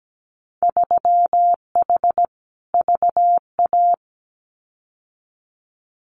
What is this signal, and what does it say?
Morse code, heard as a keyed tone, about 17 wpm, 700 Hz. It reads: 3H VA
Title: Winter Field Day Exchanges Encore 17wpm